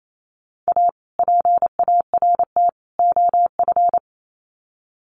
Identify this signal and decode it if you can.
Morse code, heard as a keyed tone, about 28 wpm, 700 Hz. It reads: A PART OF
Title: Sets of 3 Words 28wpm